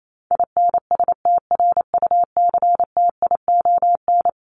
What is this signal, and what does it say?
Morse code, heard as a keyed tone, about 28 wpm, 700 Hz. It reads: INSTRUCTION
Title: Top 1751-2000 Words 28wpm